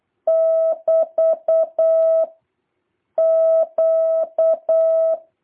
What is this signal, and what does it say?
Morse code, heard as a keyed tone, about 8 wpm, 650 Hz. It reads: <BT> Q